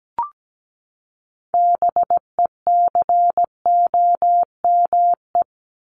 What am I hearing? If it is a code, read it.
Morse code, heard as a keyed tone, about 17 wpm, 700 Hz. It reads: BECOME
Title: Top 501-750 Words 17wpm